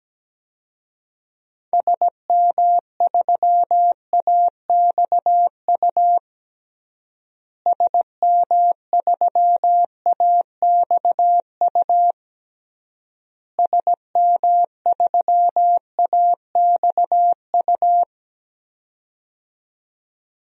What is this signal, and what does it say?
Morse code, heard as a keyed tone, about 17 wpm, 700 Hz. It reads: SM3AXU SM3AXU SM3AXU
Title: DXCC-100 Call Signs - 6 Characters - Repeated 3x - 17wpm